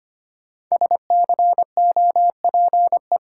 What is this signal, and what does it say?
Morse code, heard as a keyed tone, about 25 wpm, 700 Hz. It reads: SCOPE
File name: 5 Letter Words - First 3 Letters Given - 25wpm